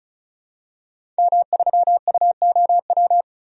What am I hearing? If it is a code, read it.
Morse code, heard as a keyed tone, about 36 wpm, 700 Hz. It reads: M3UOW
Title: DXCC-100 Call Signs - 5 Characters - 35wpm